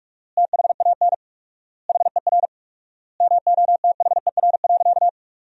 Morse code, heard as a keyed tone, about 46 wpm, 700 Hz. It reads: THAN HER MOTHER.
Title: 4x Word-Spacing - Sentences from Top 400 Words - 45wpm